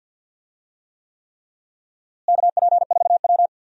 Morse code, heard as a keyed tone, about 48 wpm, 700 Hz. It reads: KP4W
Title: DXCC-100 Call Signs - 4 Characters - Repeated 3x - 50wpm